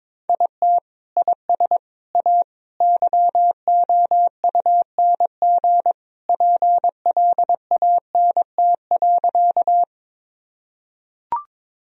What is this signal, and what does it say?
Morse code, heard as a keyed tone, about 22 wpm, 700 Hz. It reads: IT IS A YOUNG PLANT.
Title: Sentences from Top 1000 Words - Review Part 1 - 22wpm